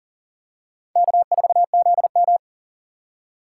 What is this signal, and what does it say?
Morse code, heard as a keyed tone, about 40 wpm, 700 Hz. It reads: K4ZM
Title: USA Call Signs - 4 Characters - 6x Character Spacing - 40wpm